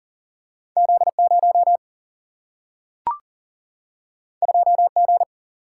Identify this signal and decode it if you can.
Morse code, heard as a keyed tone, about 40 wpm, 700 Hz. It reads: Z 0 2 G
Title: Single-Letter Single-Number 40wpm